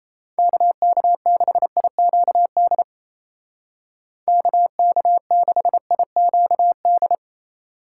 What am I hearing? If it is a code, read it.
Morse code, heard as a keyed tone, about 34 wpm, 700 Hz. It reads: KK6IQD KK6IQD
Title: Speed Racing - USA 2x3 Call Signs 28wpm